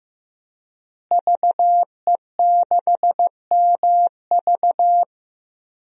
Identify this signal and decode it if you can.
Morse code, heard as a keyed tone, about 15 wpm, 700 Hz. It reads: VE6MV
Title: DXCC-100 Call Signs - 5 Characters - Repeated 3x - 15wpm